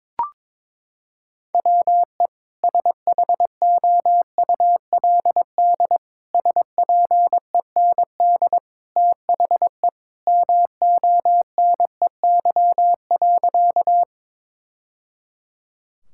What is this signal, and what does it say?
Morse code, heard as a keyed tone, about 22 wpm, 700 Hz. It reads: WE SHOULD SPEND THE MONEY.
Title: Sentences from Top 1000 Words - Review Part 5 - 22wpm